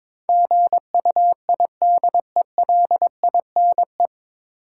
Morse code, heard as a keyed tone, about 22 wpm, 700 Hz. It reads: GUIDELINE
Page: Morse Code Ninja practice 22wpm